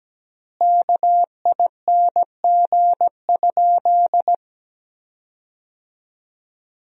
Morse code, heard as a keyed tone, about 17 wpm, 700 Hz. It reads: KING?
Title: 3x Word-Spacing - Sentences from Top 400 Words - 17wpm